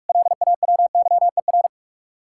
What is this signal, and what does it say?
Morse code, heard as a keyed tone, about 61 wpm, spaced as for 53 wpm, 700 Hz. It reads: LAWYER
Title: Speed-Racing Common QSO Phrases - 45wpm